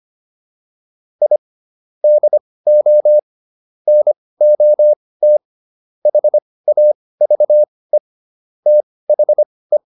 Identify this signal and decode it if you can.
Morse code, heard as a keyed tone, about 25 wpm, spaced as for 18 wpm, 600 Hz. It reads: I DO NOT HAVE THE